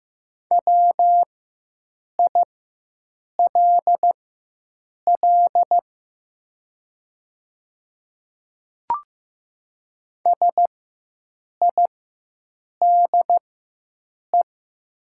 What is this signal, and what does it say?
Morse code, heard as a keyed tone, about 15 wpm, spaced as for 7 wpm, 700 Hz. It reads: WILL SIDE
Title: 4x Character Spacing - Top 200 Words - 15wpm